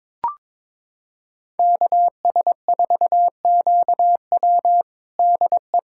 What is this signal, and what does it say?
Morse code, heard as a keyed tone, about 22 wpm, 700 Hz. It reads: KS4QW DE
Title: USA Callsign DE Callsign - 5 Characters - 22wpm